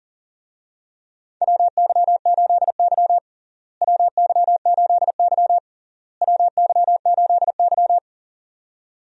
Morse code, heard as a keyed tone, about 40 wpm, 700 Hz. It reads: WY8Y WY8Y WY8Y
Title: USA Call Signs - 4 Characters - Repeated 3x - 40wpm